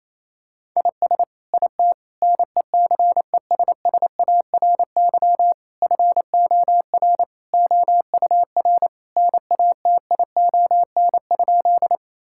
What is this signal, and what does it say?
Morse code, heard as a keyed tone, about 28 wpm, 700 Hz. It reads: IS IT NECESSARY FOR OUR NATION?
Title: Sentences from Top 1000 Words - Review Part 6 - 28wpm